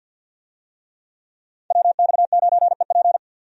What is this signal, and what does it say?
Morse code, heard as a keyed tone, about 46 wpm, 700 Hz. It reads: WX9EP
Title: USA Call Signs - 5 Characters - 50wpm